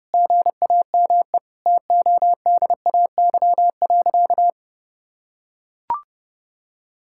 Morse code, heard as a keyed tone, about 30 wpm, 700 Hz. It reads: GAME TODAY.
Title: Sentences from Top 1600 Words 30wpm